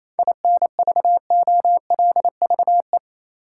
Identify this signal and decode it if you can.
Morse code, heard as a keyed tone, about 28 wpm, 700 Hz. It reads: INVOLVE
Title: Sentences from Top 1200 Words - New Word Drill Alt - 28wpm